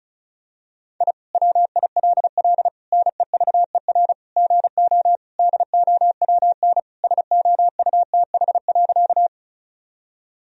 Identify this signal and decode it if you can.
Morse code, heard as a keyed tone, about 34 wpm, 700 Hz. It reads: I WILL NEVER GO DOWN SOUTH.
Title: Sentences from Top 1000 Words - Review Part 2 - 35wpm